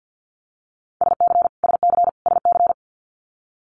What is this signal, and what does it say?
Morse code, heard as a keyed tone, about 50 wpm, 700 Hz. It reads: SC SC SC
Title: Multi-letter Phonemes Repeated 3x 50wpm